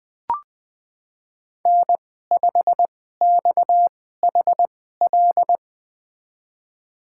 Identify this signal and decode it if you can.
Morse code, heard as a keyed tone, about 21 wpm, spaced as for 15 wpm, 700 Hz. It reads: N5XHL
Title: USA Call Signs - 2x Character Spacing - 20wpm